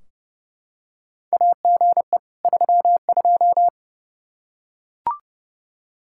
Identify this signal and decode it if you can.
Morse code, heard as a keyed tone, about 30 wpm, 700 Hz. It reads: AGE 32
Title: K1USN SST QSO Practice 30wpm